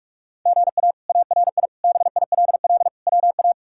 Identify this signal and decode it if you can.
Morse code, heard as a keyed tone, about 46 wpm, 700 Hz. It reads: GA ARI BILL WA
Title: K1USN SST QSO Practice - 45wpm